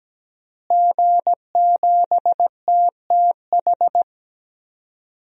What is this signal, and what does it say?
Morse code, heard as a keyed tone, about 17 wpm, 700 Hz. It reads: G7TTH
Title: DXCC-100 Call Signs - 5 Characters - Repeated 3x - 17wpm